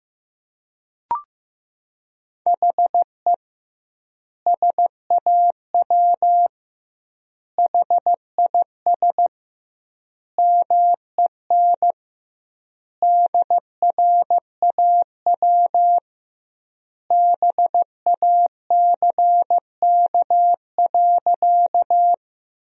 Morse code, heard as a keyed tone, about 15 wpm, 700 Hz. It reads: HE SAW HIS MEN DRAW BACK.
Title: 2x Word-Spacing - Sentences from Top 300 Words - 15wpm